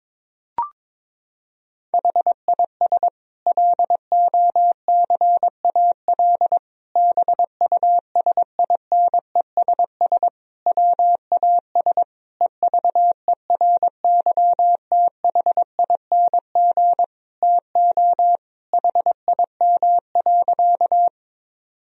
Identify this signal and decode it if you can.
Morse code, heard as a keyed tone, about 22 wpm, 700 Hz. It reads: HIS LOCAL BUSINESS WAS EVERYTHING TO HIM.
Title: Sentences from Top 1000 Words - Review Part 3 - 22wpm